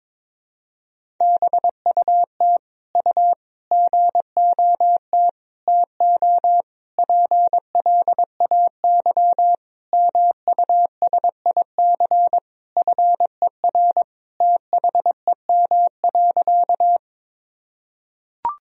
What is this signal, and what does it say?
Morse code, heard as a keyed tone, about 22 wpm, 700 Hz. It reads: BUT U GOT TO PLAY MUSIC FER THEM.
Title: Sentences from Top 300 Words and Abbreviations 22wpm